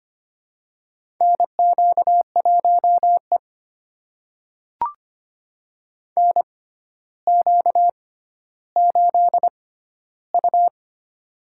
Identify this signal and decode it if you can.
Morse code, heard as a keyed tone, about 25 wpm, spaced as for 13 wpm, 700 Hz. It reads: NQ1E NQ8U
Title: USA Call Signs - 4 Characters - 6x Character Spacing - 25wpm